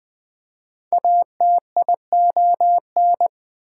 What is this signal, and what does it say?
Morse code, heard as a keyed tone, about 20 wpm, 700 Hz. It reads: ATION
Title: Prefixes and Suffixes 20wpm